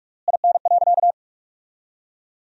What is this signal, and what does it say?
Morse code, heard as a keyed tone, about 46 wpm, 700 Hz. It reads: IN.
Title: 3x Word-Spacing - Sentences from Top 300 Words - 45wpm